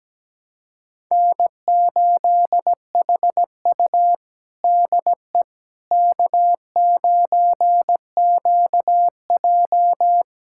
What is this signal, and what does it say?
Morse code, heard as a keyed tone, about 17 wpm, 700 Hz. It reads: N8HU DE K9QJ
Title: USA Callsign DE Callsign - 4 Characters - 17wpm